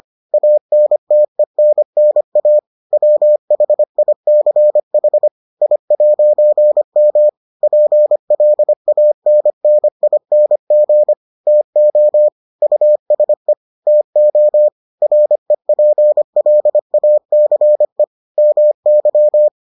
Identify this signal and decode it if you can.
Morse code, heard as a keyed tone, about 25 wpm, 600 Hz. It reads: ANTENNA WHICH I'M PLANNING TO USE TO REPLACE MY